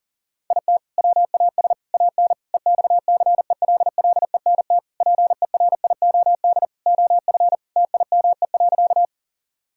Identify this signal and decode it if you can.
Morse code, heard as a keyed tone, about 40 wpm, 700 Hz. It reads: IT WAS AN EXCELLENT PERIOD OF TIME.
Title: Sentences from Top 1000 Words - Review Part 5 - 40wpm